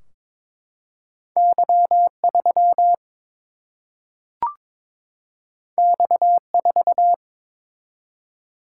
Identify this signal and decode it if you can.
Morse code, heard as a keyed tone, about 22 wpm, 700 Hz. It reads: Y 3 X 4
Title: Single-Letter Single-Number 22wpm